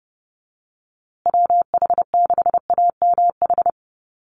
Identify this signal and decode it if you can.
Morse code, heard as a keyed tone, about 30 wpm, 700 Hz. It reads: WH6AMH
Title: USA Call Signs Repeated 3x 30wpm